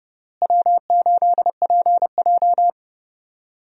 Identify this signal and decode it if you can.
Morse code, heard as a keyed tone, about 31 wpm, 700 Hz. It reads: W8PJ
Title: USA Call Signs - 4 Characters - Repeated 3x - 30wpm